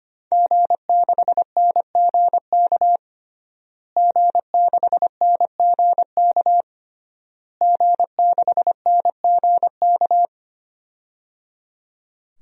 Morse code, heard as a keyed tone, about 25 wpm, 700 Hz. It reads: G6NGK G6NGK G6NGK
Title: DXCC-100 Call Signs - 5 Characters - Repeated 3x - 25wpm